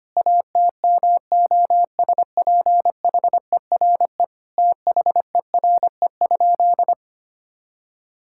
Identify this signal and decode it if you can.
Morse code, heard as a keyed tone, about 25 wpm, 700 Hz. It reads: ATMOSPHERE THERE?
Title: Sentences from Top 1600 Words 25wpm